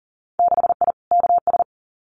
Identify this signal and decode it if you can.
Morse code, heard as a keyed tone, about 40 wpm, 700 Hz. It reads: 6I KS
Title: Winter Field Day Exchanges Encore 40wpm